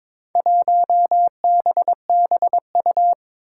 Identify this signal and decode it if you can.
Morse code, heard as a keyed tone, about 23 wpm, 700 Hz. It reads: 1BBU
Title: Four Challenging Letter-Number Combinations 22wpm